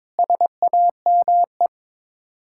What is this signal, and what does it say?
Morse code, heard as a keyed tone, about 22 wpm, 700 Hz. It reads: SAME
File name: practice / 22wpm / Build A Phrase - 5-Word Phrases - 22wpm